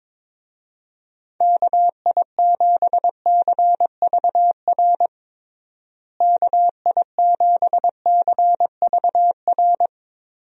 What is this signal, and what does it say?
Morse code, heard as a keyed tone, about 22 wpm, 700 Hz. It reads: KI7CVR KI7CVR
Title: USA Call Signs - 6 Characters - Repeated 2x - 22wpm